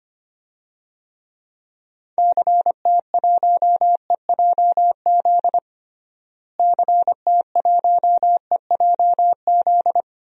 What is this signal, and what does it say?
Morse code, heard as a keyed tone, about 25 wpm, 700 Hz. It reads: CT1EJZ CT1EJZ
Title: DXCC-100 Call Signs - 6 Characters - Repeated 2x - 25wpm